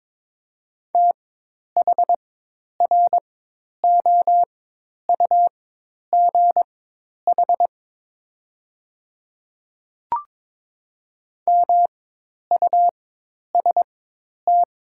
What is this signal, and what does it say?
Morse code, heard as a keyed tone, about 22 wpm, spaced as for 10 wpm, 700 Hz. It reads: THROUGH MUST
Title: 4x Character Spacing - Top 200 Words - 22wpm